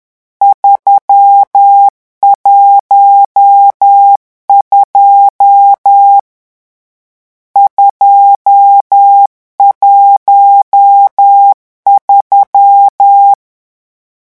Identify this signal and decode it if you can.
Morse code, heard as a keyed tone, about 10 wpm, 800 Hz. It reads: 312 213
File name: morse mp3 / lecon10-vitesse 10